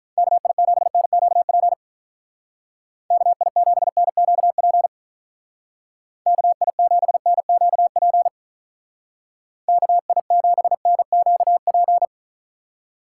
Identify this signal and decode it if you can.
Morse code, heard as a keyed tone, about 44 wpm, 700 Hz. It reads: KI7NQP KI7NQP KI7NQP KI7NQP